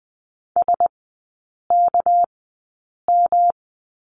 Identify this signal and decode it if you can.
Morse code, heard as a keyed tone, about 20 wpm, spaced as for 8 wpm, 690 Hz. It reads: SKM
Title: 198 Lesson 32e - Letter k - Three Characters 20wpm